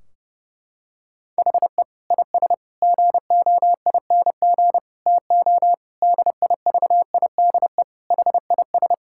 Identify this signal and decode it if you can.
Morse code, heard as a keyed tone, about 30 wpm, 700 Hz. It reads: HE IS GOING TO DIVIDE HIS
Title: Sentences from Top 1600 Words 30wpm